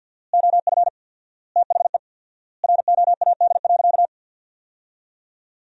Morse code, heard as a keyed tone, about 50 wpm, 700 Hz. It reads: OF THE ROAD.
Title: 4x Word-Spacing - Sentences from Top 400 Words - 50wpm